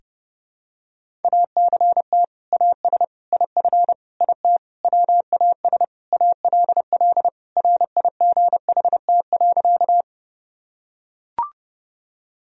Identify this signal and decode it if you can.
Morse code, heard as a keyed tone, about 30 wpm, 700 Hz. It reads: ACT AS IF IT WAS ALL RIGHT.
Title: Sentences from Top 1000 Words - Review Part 1 - 30wpm